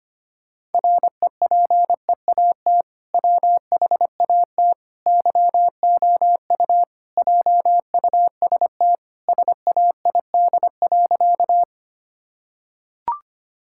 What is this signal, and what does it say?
Morse code, heard as a keyed tone, about 25 wpm, 700 Hz. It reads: REPEAT WHAT YOU JUST SAID.
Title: Sentences from Top 1600 Words 25wpm